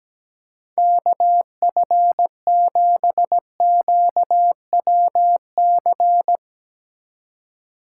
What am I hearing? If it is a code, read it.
Morse code, heard as a keyed tone, about 17 wpm, 700 Hz. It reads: KF7QWC
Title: USA Call Signs - 6 Characters - 2x Character Spacing - 17wpm